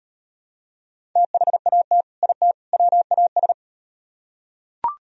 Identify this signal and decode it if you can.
Morse code, heard as a keyed tone, about 36 wpm, 700 Hz. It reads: THAT IT WAS